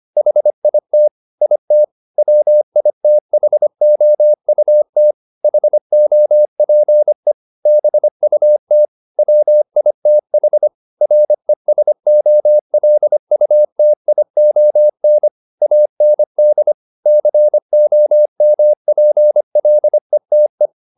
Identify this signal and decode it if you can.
Morse code, heard as a keyed tone, about 25 wpm, 600 Hz. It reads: HIT IT WITHOUT HOPE BUT WITH RESOLUTION AND COMPLETE